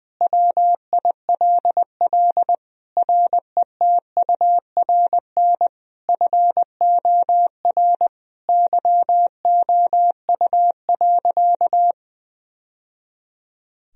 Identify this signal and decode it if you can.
Morse code, heard as a keyed tone, about 20 wpm, 700 Hz. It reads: WILL RETURN FOR YOU.
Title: Sentences from Top 1000 Words - Review Part 1 - 20wpm